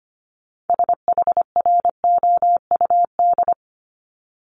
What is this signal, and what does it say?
Morse code, heard as a keyed tone, about 25 wpm, 700 Hz. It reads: SHROUD
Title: Multi-letter Phonemes Repeated 3x 25wpm